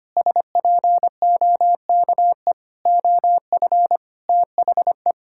Morse code, heard as a keyed tone, about 25 wpm, 700 Hz. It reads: SPOKE OF THE
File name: practice / 25wpm / Sentences with 2-Word Phrases - 25wpm